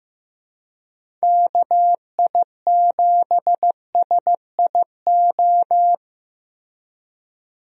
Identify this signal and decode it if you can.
Morse code, heard as a keyed tone, about 15 wpm, 700 Hz. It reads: KI7SIO
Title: USA Call Signs - 6 Characters - Repeated 2x - 15wpm